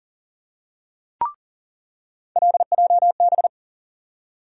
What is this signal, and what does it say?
Morse code, heard as a keyed tone, about 40 wpm, 700 Hz. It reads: LJB
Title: USA Call Sign Postfixes - 3 Letters 40wpm